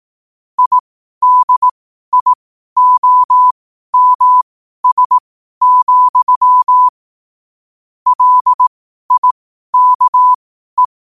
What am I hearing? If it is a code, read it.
Morse code, heard as a keyed tone, about 18 wpm, spaced as for 12 wpm, 1000 Hz. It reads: IDIOMS, LIKE
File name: qotdfiles / 12wpm